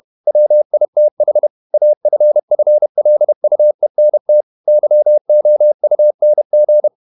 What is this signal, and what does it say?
Morse code, heard as a keyed tone, about 31 wpm, 600 Hz. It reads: WITH AFFLUENT YOUNG
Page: Never 31wpm | CW med Gnister